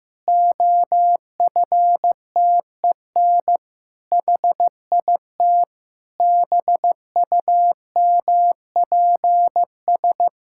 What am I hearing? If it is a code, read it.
Morse code, heard as a keyed tone, about 15 wpm, 700 Hz. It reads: OFTEN HIT BUMPS
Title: Pangram Challenge v2 - 15wpm